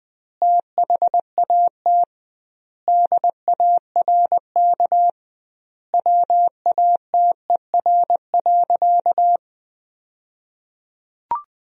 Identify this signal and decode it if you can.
Morse code, heard as a keyed tone, about 20 wpm, 700 Hz. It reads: THAT DARK WATER.
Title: 2x Word-Spacing - Sentences from Top 500 Words - 20wpm